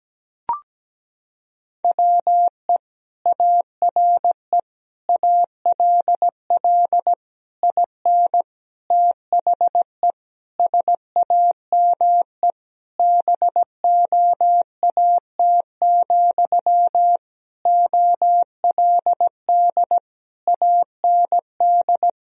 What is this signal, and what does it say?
Morse code, heard as a keyed tone, about 17 wpm, 700 Hz. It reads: WE ARE ALL IN THE SAME BOAT, OLD AND
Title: Sentences from Top 1000 Words - Review Part 3 - 17wpm